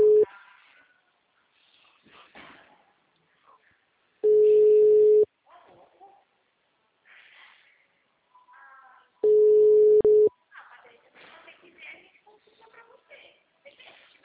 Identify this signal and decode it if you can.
Ringback tone (Europe, 425 Hz), with speech.